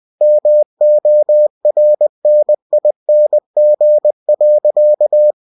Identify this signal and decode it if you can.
Morse code, heard as a keyed tone, about 20 wpm, 600 Hz. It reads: MORNING.
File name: cw-books / winnie-the-pooh / winnie-20wpm-page-0110